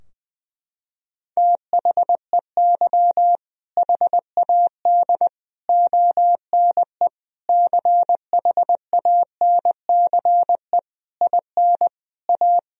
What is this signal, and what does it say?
Morse code, heard as a keyed tone, about 20 wpm, 700 Hz. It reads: THEY HAD ONE CHANCE IN A
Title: Sentences from Top 1000 Words - Review Part 3 - 20wpm